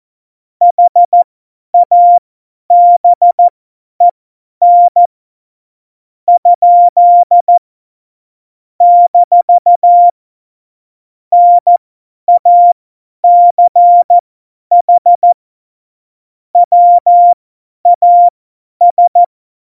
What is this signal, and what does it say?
Morse code, heard as a keyed tone, about 14 wpm, spaced as for 9 wpm, 700 Hz. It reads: HABEN? - NACH WAS
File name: Morsekurs / Witze / Witz 18. Juni 2019, 50 F70